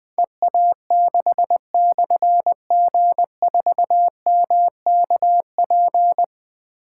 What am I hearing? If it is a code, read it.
Morse code, heard as a keyed tone, about 21 wpm, 700 Hz. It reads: EA6/G4MKP
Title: Difficult Call Signs 20wpm